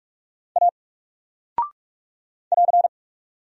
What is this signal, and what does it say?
Morse code, heard as a keyed tone, about 44 wpm, 700 Hz. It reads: A <AR>
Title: Single Character (Letter:Number:Punctuation:Prosign) 45wpm